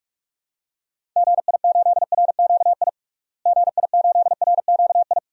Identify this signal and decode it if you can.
Morse code, heard as a keyed tone, about 46 wpm, 700 Hz. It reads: GI8RQI GI8RQI
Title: DXCC-100 Call Signs - 6 Characters - Repeated 2x - 45wpm